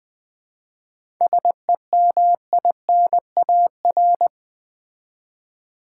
Morse code, heard as a keyed tone, about 20 wpm, 700 Hz. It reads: SEMINAR
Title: 7 Letter Words - First 3 Letters Given - 20wpm